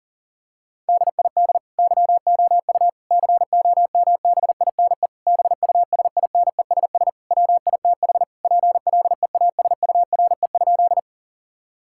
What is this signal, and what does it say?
Morse code, heard as a keyed tone, about 40 wpm, 700 Hz. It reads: DID YOU COMBINE BUSINESS WITH PLEASURE?
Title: Sentences from Top 1600 Words 40wpm